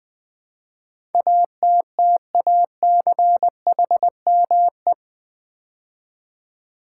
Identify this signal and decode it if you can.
Morse code, heard as a keyed tone, about 20 wpm, 700 Hz. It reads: ATTACHME
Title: Build A Word - 4 Letters At A Time - 10-Letter Words - 20wpm